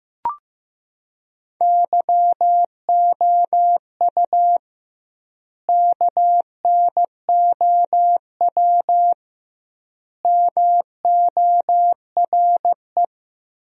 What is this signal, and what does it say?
Morse code, heard as a keyed tone, about 16 wpm, 700 Hz. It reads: YOU KNOW MORE
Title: 2x Word-Spacing - Sentences from Top 100 Words - 15wpm